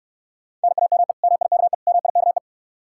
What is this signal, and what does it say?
Morse code, heard as a keyed tone, about 69 wpm, 700 Hz. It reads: NAME MIKE MIKE